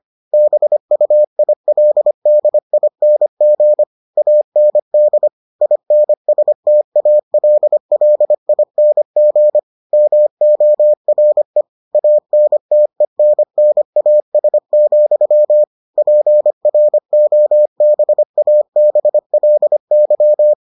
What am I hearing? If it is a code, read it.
Morse code, heard as a keyed tone, about 25 wpm, 600 Hz. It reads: BUILDING AND INSTALLING MORE ANTENNAS, PROBABLY